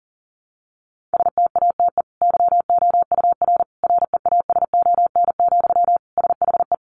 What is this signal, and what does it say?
Morse code, heard as a keyed tone, about 40 wpm, 700 Hz. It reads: STATE YOUR REASON, SHE
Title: Sentences from Top 700 Words 40wpm